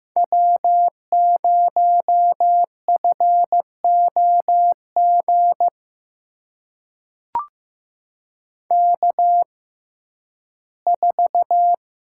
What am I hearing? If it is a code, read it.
Morse code, heard as a keyed tone, about 15 wpm, spaced as for 11 wpm, 700 Hz. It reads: W0FOG K4
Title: USA Call Signs - 5 Characters - 6x Character Spacing - 15wpm